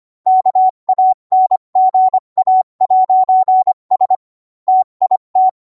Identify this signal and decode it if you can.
Morse code, heard as a keyed tone, about 25 wpm, 750 Hz. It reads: KANGA'S TIT